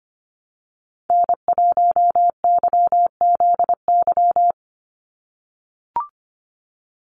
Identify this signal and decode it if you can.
Morse code, heard as a keyed tone, about 25 wpm, 700 Hz. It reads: N1YZY
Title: USA Call Signs 25wpm